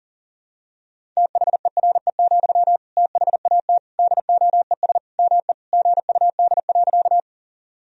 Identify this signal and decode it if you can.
Morse code, heard as a keyed tone, about 40 wpm, 700 Hz. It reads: THERE, THAT DOES ME GUD.
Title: Sentences from Top 200 Words and Abbreviations 40wpm